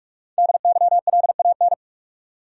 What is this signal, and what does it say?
Morse code, heard as a keyed tone, about 45 wpm, 700 Hz. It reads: DYLAN
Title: USA Common Names - Max 5 Letters - 45wpm